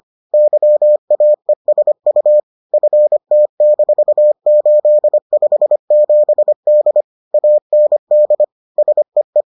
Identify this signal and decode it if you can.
Morse code, heard as a keyed tone, about 25 wpm, 600 Hz. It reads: YAESU FT-857D AND SEE